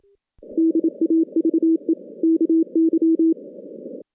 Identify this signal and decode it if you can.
Morse code, heard as a keyed tone, about 27 wpm, 330 Hz. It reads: DAVE KY
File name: 2016 JAN CW NAQP / 2016 Jan naqp k5ot - 0-10